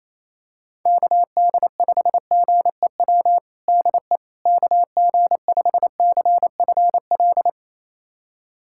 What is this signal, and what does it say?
Morse code, heard as a keyed tone, about 28 wpm, 700 Hz. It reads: KD5GEW DE KG5CFL
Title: USA Callsign DE Callsign - 6 Characters - 28wpm